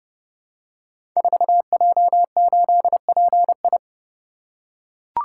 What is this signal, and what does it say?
Morse code, heard as a keyed tone, about 31 wpm, 700 Hz. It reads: 4J8PI
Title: Code Groups - 5 Letter:Numbers - 30wpm